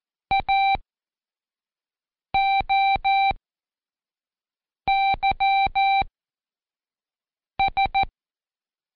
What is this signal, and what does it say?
Morse code, heard as a keyed tone, about 14 wpm, spaced as for 5 wpm, 780 Hz. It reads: AOYS
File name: Code Quick / AudioDownld1 / Seg18-Day2-Quiz ASYO